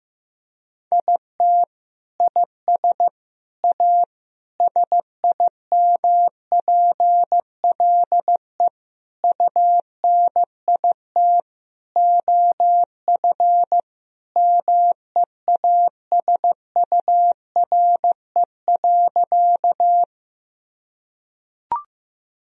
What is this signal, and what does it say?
Morse code, heard as a keyed tone, about 15 wpm, 700 Hz. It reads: IT IS A SIMPLE UNIT OF MEASURE.
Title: Sentences from Top 1000 Words - Review Part 2 - 15wpm